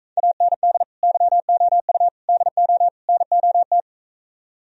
Morse code, heard as a keyed tone, about 45 wpm, 700 Hz. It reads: AND YOU DO NOT